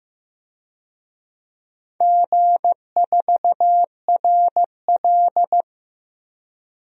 Morse code, heard as a keyed tone, about 16 wpm, 700 Hz. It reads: G4RL
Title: DXCC-100 Call Signs - 4 Characters - Repeated 3x - 15wpm